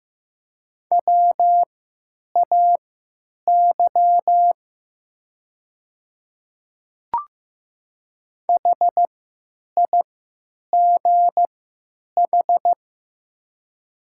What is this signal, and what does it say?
Morse code, heard as a keyed tone, about 15 wpm, spaced as for 9 wpm, 700 Hz. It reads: WAY HIGH
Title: 3x Character Spacing - Top 200 Words - 15wpm